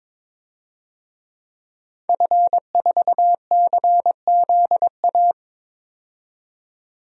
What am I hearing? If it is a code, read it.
Morse code, heard as a keyed tone, about 22 wpm, 700 Hz. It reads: F4CZA
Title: DXCC-100 Call Signs - 5 Characters - 22wpm